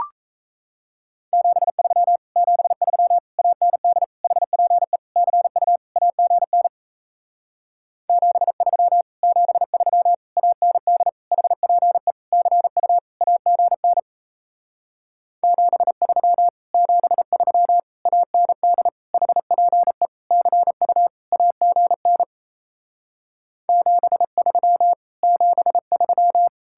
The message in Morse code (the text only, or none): 73 73 AND HPE CU AGN 73 73 AND HPE CU AGN 73 73 AND HPE CU AGN 73 73